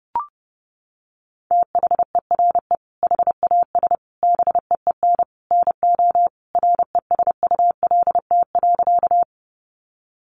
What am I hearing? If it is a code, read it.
Morse code, heard as a keyed tone, about 30 wpm, 700 Hz. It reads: THERE HAS BEEN NO RESULT.
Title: Sentences from Top 700 Words 30wpm